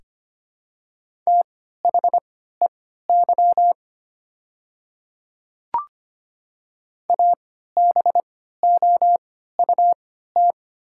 Morse code, heard as a keyed tone, about 25 wpm, spaced as for 14 wpm, 700 Hz. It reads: THEY ABOUT
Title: 3x Character Spacing - Top 100 Words - 25wpm